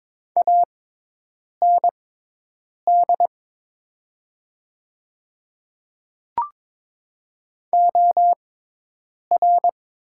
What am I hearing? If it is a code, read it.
Morse code, heard as a keyed tone, about 22 wpm, spaced as for 8 wpm, 690 Hz. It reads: AND OR
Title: 6x Character Spacing - Top 300 Words - 22wpm